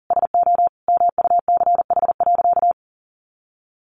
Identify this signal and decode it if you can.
Morse code, heard as a keyed tone, about 40 wpm, 700 Hz. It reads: SO MUCH.
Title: Sentences from Top 1400 Words 40wpm